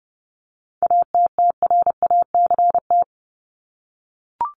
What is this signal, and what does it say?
Morse code, heard as a keyed tone, about 30 wpm, 700 Hz. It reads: ATTRACT
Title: Top 1501-1750 Words 30wpm